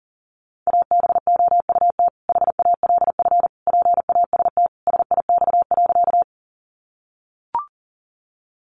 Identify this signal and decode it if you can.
Morse code, heard as a keyed tone, about 40 wpm, 700 Hz. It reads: ABOUT HALF PAST SIX.
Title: Sentences from Top 600 Words 40wpm